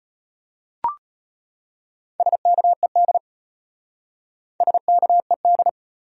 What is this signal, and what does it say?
Morse code, heard as a keyed tone, about 33 wpm, 700 Hz. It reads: SKED SKED